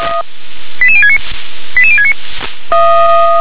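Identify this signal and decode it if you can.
Five-tone selective call (ZVEI), heard as quick runs of five tones, tones of about 70 ms. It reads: 88069 88069
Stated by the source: ZVEI five-tone selective call received on SDR.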